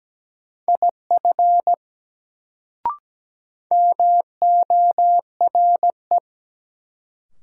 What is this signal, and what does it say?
Morse code, heard as a keyed tone, about 17 wpm, 700 Hz. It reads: IF MORE
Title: Sets of 2 Words - New Word Drill - 17wpm